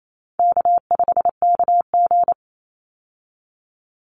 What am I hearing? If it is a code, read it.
Morse code, heard as a keyed tone, about 28 wpm, 700 Hz. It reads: K5KG
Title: Easy Call Signs - 28wpm